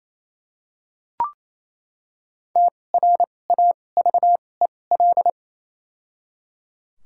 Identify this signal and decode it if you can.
Morse code, heard as a keyed tone, about 28 wpm, spaced as for 20 wpm, 700 Hz. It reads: TRAVEL
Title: 2x Character Spacing - Top 500 Words - 28wpm